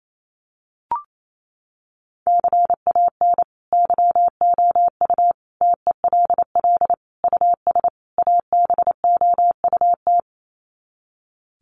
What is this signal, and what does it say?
Morse code, heard as a keyed tone, about 28 wpm, 700 Hz. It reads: CAN YOU TELL US ABOUT
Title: Sets of 5 Words Encore2 28wpm